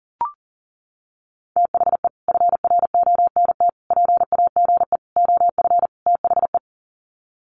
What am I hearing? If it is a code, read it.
Morse code, heard as a keyed tone, about 40 wpm, 700 Hz. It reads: THE FRONT PAGE OF THE
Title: Sets of 5 Words Encore 40wpm